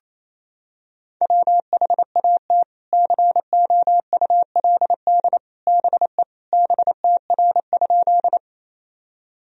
Morse code, heard as a keyed tone, about 28 wpm, 700 Hz. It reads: WHAT COULD BE BTR?
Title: Sentences from Top 400 Words and Abbreviations 28wpm